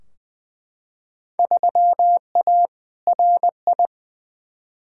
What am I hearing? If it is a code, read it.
Morse code, heard as a keyed tone, about 20 wpm, 700 Hz. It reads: 3A RI
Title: Field Day Exchanges - 20wpm